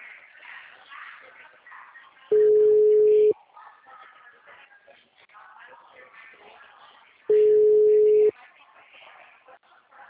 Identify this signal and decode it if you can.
Ringback tone (Europe, 425 Hz).
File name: out- -2038- - - .21767